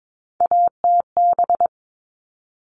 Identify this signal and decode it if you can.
Morse code, heard as a keyed tone, about 22 wpm, 700 Hz. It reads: ATB
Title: All Cut Numbers - 3-Digit 22wpm